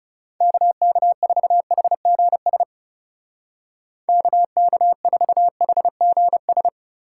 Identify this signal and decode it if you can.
Morse code, heard as a keyed tone, about 35 wpm, 700 Hz. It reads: KK4HGS KK4HGS
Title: Speed Racing - USA 2x3 Call Signs 30wpm